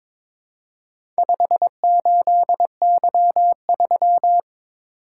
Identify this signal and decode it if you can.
Morse code, heard as a keyed tone, about 22 wpm, 690 Hz. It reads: 58Y3
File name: practice / 22wpm / Four Letter-Number Combinations - 22wpm